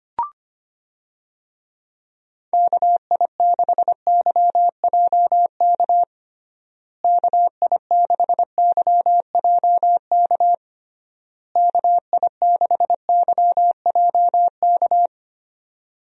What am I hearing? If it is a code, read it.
Morse code, heard as a keyed tone, about 25 wpm, 700 Hz. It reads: KI6YJK KI6YJK KI6YJK
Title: USA Call Signs - 6 Characters - Repeated 3x - 25wpm